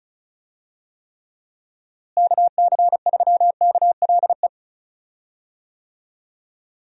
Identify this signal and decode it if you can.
Morse code, heard as a keyed tone, about 34 wpm, 700 Hz. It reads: KC3KLE
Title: USA Call Signs - 6 Characters - Repeated 3x - 35wpm